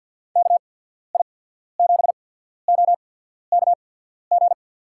Morse code, heard as a keyed tone, about 53 wpm, spaced as for 14 wpm, 700 Hz. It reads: KI7OKG
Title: USA Call Signs - 6 Characters - 8x Character Spacing - 50wpm